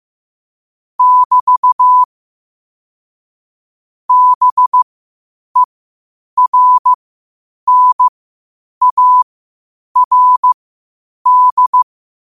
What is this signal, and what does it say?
Morse code, heard as a keyed tone, about 15 wpm, spaced as for 8 wpm, 1000 Hz. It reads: <BT> BERNARD
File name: qotdfiles / 8wpm